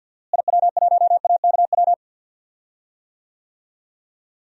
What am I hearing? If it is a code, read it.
Morse code, heard as a keyed tone, about 52 wpm, 700 Hz. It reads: IW1AKW
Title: DXCC-100 Call Signs - 6 Characters - Repeated 3x - 50wpm